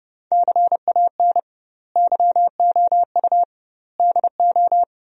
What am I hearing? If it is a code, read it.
Morse code, heard as a keyed tone, about 30 wpm, 700 Hz. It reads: CAN YOU DO
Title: 2x Word-Spacing - Sentences from Top 100 Words - 30wpm